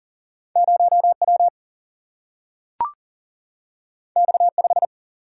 Morse code, heard as a keyed tone, about 40 wpm, 700 Hz. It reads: 0W X5
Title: Single-Letter Single-Number 40wpm